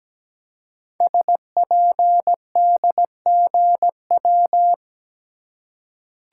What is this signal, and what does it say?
Morse code, heard as a keyed tone, about 17 wpm, 700 Hz. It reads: SPDGW
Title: Code Groups - 5 Letters - 17wpm